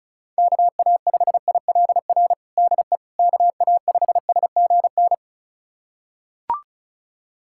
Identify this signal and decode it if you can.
Morse code, heard as a keyed tone, about 35 wpm, 700 Hz. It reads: KA5ILR DE KA5SGN
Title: USA Callsign DE Callsign - 6 Characters - 35wpm